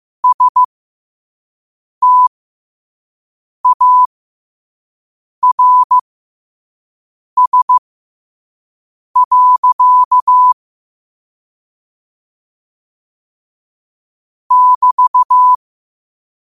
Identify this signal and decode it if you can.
Morse code, heard as a keyed tone, about 15 wpm, spaced as for 5 wpm, 1000 Hz. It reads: STARS. <BT>
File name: qotdfiles / 5wpm